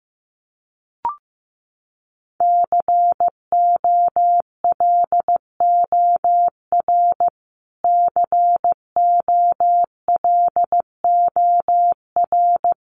Morse code, heard as a keyed tone, about 15 wpm, 700 Hz. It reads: COLOR COLOR
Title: Top 500 Words Repeated 15wpm